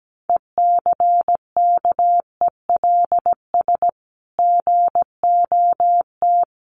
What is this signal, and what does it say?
Morse code, heard as a keyed tone, about 17 wpm, 700 Hz. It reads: ECKELS GOT
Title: A Sound Of Thunder 17wpm